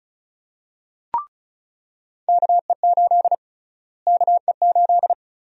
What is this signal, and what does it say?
Morse code, heard as a keyed tone, about 36 wpm, 700 Hz. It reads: KE8 KE8
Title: USA Call Sign Prefixes - Repeated 2x - 35wpm